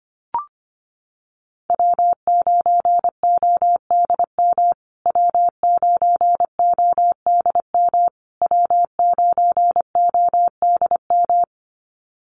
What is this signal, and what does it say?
Morse code, heard as a keyed tone, about 25 wpm, 700 Hz. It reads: W9ODM W9ODM W9ODM
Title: USA Call Signs Repeated 3x 25wpm